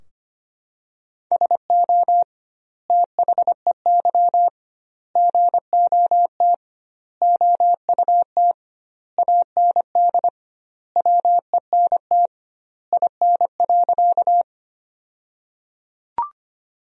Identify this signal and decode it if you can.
Morse code, heard as a keyed tone, about 25 wpm, 700 Hz. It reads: SO THEY GOT OUT AND WENT IN.
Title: 2x Word-Spacing - Sentences from Top 300 Words - 25wpm